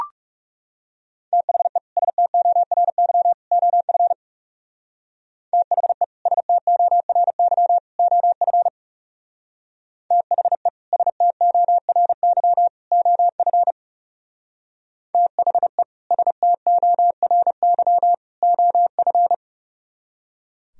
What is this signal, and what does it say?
Morse code, heard as a keyed tone, about 36 wpm, 700 Hz. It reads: THE STORY OF THE STORY OF THE STORY OF THE STORY OF
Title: Speed-Racing - Sets of 3 Words 30wpm